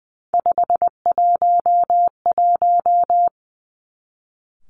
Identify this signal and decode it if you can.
Morse code, heard as a keyed tone, about 19 wpm, 700 Hz. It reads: 511
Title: Three Challenging Letter-Number Combinations 20wpm